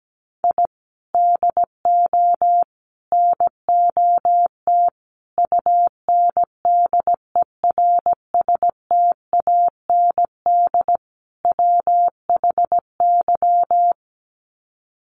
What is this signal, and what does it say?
Morse code, heard as a keyed tone, about 17 wpm, 700 Hz. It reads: I DO NOT UNDERSTAND WHY
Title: Sets of 5 Words Encore 17wpm